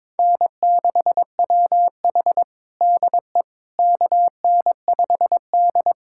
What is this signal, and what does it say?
Morse code, heard as a keyed tone, about 22 wpm, 700 Hz. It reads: N6WH DE KN5D
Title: USA Callsign DE Callsign - 4 Characters - 22wpm